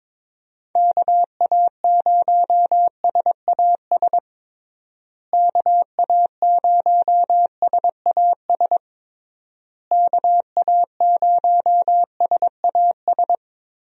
Morse code, heard as a keyed tone, about 22 wpm, 700 Hz. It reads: KA0SAS KA0SAS KA0SAS
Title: USA Call Signs - 6 Characters - Repeated 3x - 22wpm